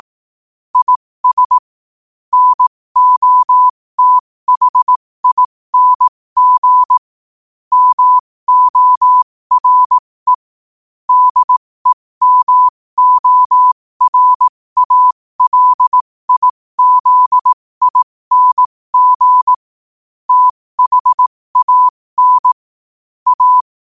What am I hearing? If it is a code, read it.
Morse code, heard as a keyed tone, about 18 wpm, spaced as for 15 wpm, 1000 Hz. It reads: IS NOTHING MORE DEMORALIZING THAN A
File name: qotdfiles / 15wpm